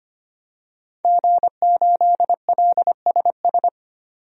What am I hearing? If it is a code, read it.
Morse code, heard as a keyed tone, about 25 wpm, 700 Hz. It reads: G8LSS
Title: DXCC-100 Call Signs - 5 Characters - 25wpm